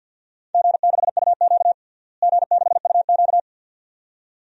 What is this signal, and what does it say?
Morse code, heard as a keyed tone, about 51 wpm, 700 Hz. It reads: G6UQ G6UQ
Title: DXCC-100 Call Signs - 4 Characters - Repeated 2x - 50wpm